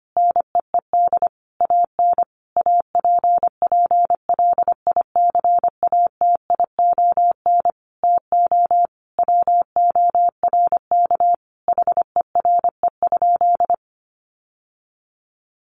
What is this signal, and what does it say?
Morse code, heard as a keyed tone, about 25 wpm, 700 Hz. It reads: NEED AN APPLICATION TO WORK HERE?
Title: Sentences from Top 1200 Words 25wpm